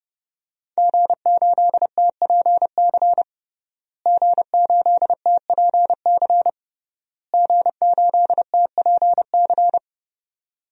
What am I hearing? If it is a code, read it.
Morse code, heard as a keyed tone, about 30 wpm, 700 Hz. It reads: G8TPC G8TPC G8TPC
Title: DXCC-100 Call Signs - 5 Characters - Repeated 3x - 30wpm